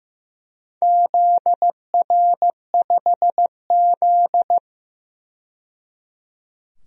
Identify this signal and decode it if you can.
Morse code, heard as a keyed tone, about 15 wpm, 700 Hz. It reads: ZR5Z
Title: DXCC-100 Call Signs - 4 Characters - 15wpm